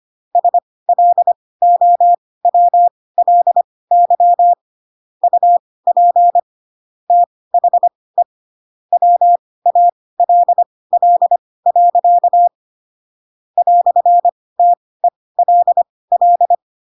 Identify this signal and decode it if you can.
Morse code, heard as a keyed tone, about 25 wpm, spaced as for 18 wpm, 700 Hz. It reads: SLOWLY UP THE WALL. "TELL
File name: Pinocchio / Pinocchio ch4 12 25 ik1hge